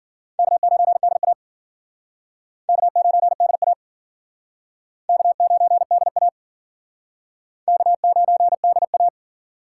Morse code, heard as a keyed tone, about 51 wpm, 700 Hz. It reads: K9DA K9DA K9DA K9DA